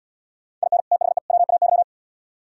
Morse code, heard as a keyed tone, about 70 wpm, 700 Hz. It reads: IN THE WAY